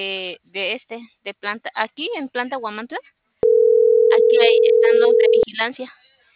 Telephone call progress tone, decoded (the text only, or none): ringback tone